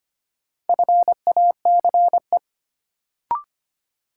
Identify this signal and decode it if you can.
Morse code, heard as a keyed tone, about 25 wpm, 700 Hz. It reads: FACE
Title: Sets of 3 Words - New Word Drill - 25wpm